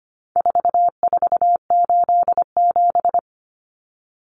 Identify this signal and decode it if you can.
Morse code, heard as a keyed tone, about 25 wpm, 700 Hz. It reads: 4487
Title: Numbers - 4-Digit 25wpm